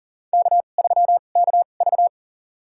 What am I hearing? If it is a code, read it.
Morse code, heard as a keyed tone, about 40 wpm, spaced as for 29 wpm, 700 Hz. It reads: K3KV
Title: USA Call Signs - 2x Character Spacing - 40wpm